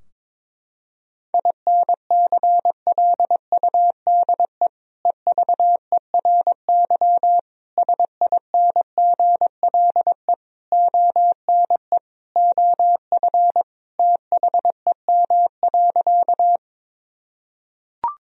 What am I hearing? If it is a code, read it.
Morse code, heard as a keyed tone, about 22 wpm, 700 Hz. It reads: INCLUDE EVERY SINGLE ONE OF THEM.
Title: Sentences from Top 1000 Words - Review Part 4 - 22wpm